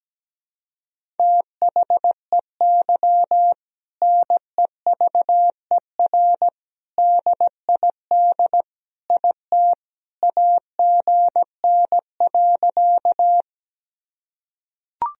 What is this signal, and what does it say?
Morse code, heard as a keyed tone, about 17 wpm, 700 Hz. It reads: THEY NEVER DID IT AGN.
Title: Sentences from Top 300 Words and Abbreviations 17wpm